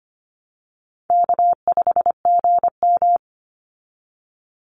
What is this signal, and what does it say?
Morse code, heard as a keyed tone, about 25 wpm, 700 Hz. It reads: K5GM
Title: Call Signs Repeated 3x - 25wpm